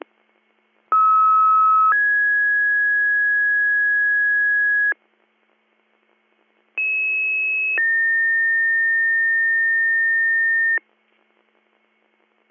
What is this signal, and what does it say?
Two-tone page, heard as a one-second tone, then a three-second tone, about 1290 then 1650 Hz, about 2600 then 1800 Hz. It is heard from 0.9 to 10.8 s.